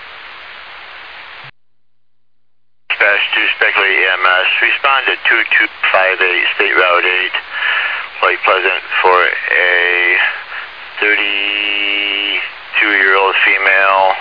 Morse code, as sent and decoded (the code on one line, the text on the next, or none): none